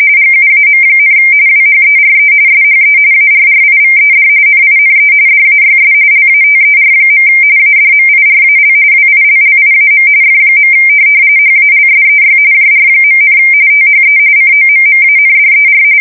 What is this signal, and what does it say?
RTTY, about 45.45 baud. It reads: GUAM, WHERE SHE ARRIVED NINE DAYS LATER. ADMIRAL WILLIAM F. HALSEY JR., THE COMMANDER
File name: fsk170 / USS-Missouri